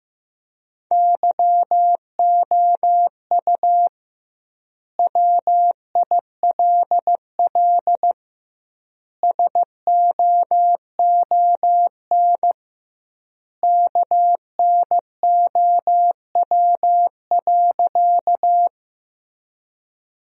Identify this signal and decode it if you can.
Morse code, heard as a keyed tone, about 16 wpm, 700 Hz. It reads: YOU WILL SOON KNOW.
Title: 2x Word-Spacing - Sentences from Top 400 Words - 15wpm